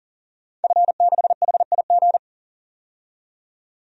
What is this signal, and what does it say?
Morse code, heard as a keyed tone, about 40 wpm, 700 Hz. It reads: F6HIG
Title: DXCC-100 Call Signs - 5 Characters - 40wpm